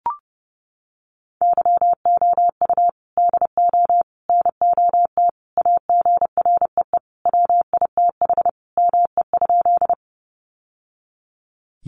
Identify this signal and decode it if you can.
Morse code, heard as a keyed tone, about 30 wpm, 700 Hz. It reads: YOU DO NOT AGREE WITH ME?
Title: Sentences from Top 1100 Words 30wpm